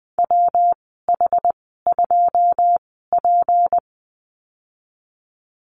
Morse code, heard as a keyed tone, about 20 wpm, spaced as for 14 wpm, 700 Hz. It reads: WH2P
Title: 231 Lesson 37g - Letter x - Call Signs 20wpm